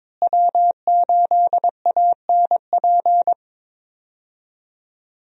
Morse code, heard as a keyed tone, about 22 wpm, 700 Hz. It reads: W8ANP
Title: USA Call Signs - 5 Characters - Repeated 2x - 22wpm